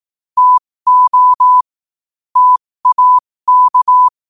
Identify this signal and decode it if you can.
Morse code, heard as a keyed tone, about 18 wpm, spaced as for 15 wpm, 1000 Hz. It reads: TO TAK
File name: qotdfiles / 15wpm